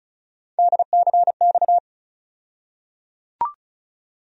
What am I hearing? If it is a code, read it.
Morse code, heard as a keyed tone, about 34 wpm, 700 Hz. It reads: DCX
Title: USA Call Sign Postfixes - 3 Letters 35wpm